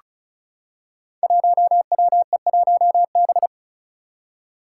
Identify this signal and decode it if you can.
Morse code, heard as a keyed tone, about 36 wpm, 700 Hz. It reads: 1WE1B
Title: Five Challenging Letter-Number Combinations 35wpm